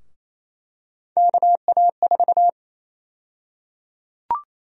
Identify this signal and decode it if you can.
Morse code, heard as a keyed tone, about 28 wpm, 700 Hz. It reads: KA4
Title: USA Call Sign Prefixes - Repeated 2x - 28wpm